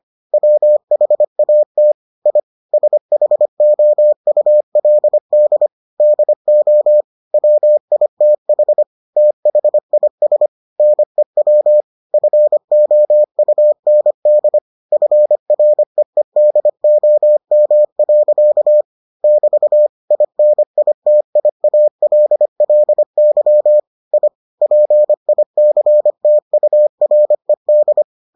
Morse code, heard as a keyed tone, about 25 wpm, 600 Hz. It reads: WHAT I SHOULD DO WITH THIS NEW FOUND FREEDOM. <BT> INITIALLY I PICTURED